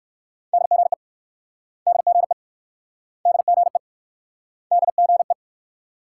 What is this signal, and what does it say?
Morse code, heard as a keyed tone, about 64 wpm, 690 Hz. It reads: DGE DGE DGE DGE